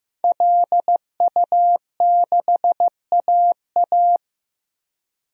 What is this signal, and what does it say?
Morse code, heard as a keyed tone, about 15 wpm, 700 Hz. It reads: LU6AA
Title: DXCC-100 Call Signs - 5 Characters - Repeated 3x - 15wpm